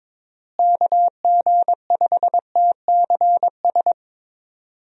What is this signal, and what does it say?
Morse code, heard as a keyed tone, about 22 wpm, 700 Hz. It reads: KG5TCS
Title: USA Call Signs - 6 Characters - Repeated 2x - 22wpm